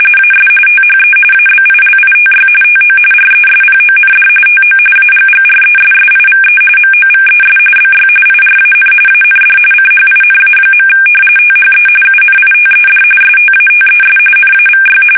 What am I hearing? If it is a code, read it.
RTTY, about 45.45 baud. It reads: AS THE ABILITY TO RESPOND TO AGGRESSIVE MOVES BY IRAN AND CHINA. THE UNITED STATES NAVY
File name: fsk850 / usnavy1